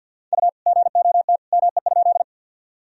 Morse code, heard as a keyed tone, about 50 wpm, 700 Hz. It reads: U GOT ME?
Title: Sentences from Top 300 Words and Abbreviations 50wpm